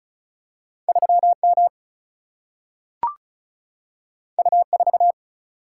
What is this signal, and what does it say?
Morse code, heard as a keyed tone, about 36 wpm, 700 Hz. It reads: 3 M U 4
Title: Single-Letter Single-Number 35wpm